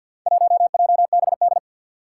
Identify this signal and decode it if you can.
Morse code, heard as a keyed tone, about 53 wpm, 700 Hz. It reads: 1JBD
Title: Four Challenging Letter-Number Combinations 50wpm